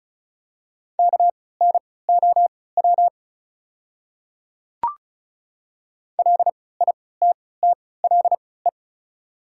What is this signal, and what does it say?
Morse code, heard as a keyed tone, about 36 wpm, spaced as for 20 wpm, 700 Hz. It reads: KNOW LITTLE
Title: 3x Character Spacing - Top 200 Words - 35wpm